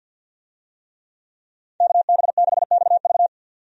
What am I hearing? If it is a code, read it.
Morse code, heard as a keyed tone, about 52 wpm, 700 Hz. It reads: KB6XV
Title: USA Call Signs - 5 Characters - 50wpm